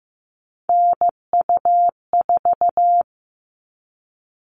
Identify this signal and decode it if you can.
Morse code, heard as a keyed tone, about 15 wpm, 700 Hz. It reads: NU4
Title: All Cut Numbers - 3-Digit 15wpm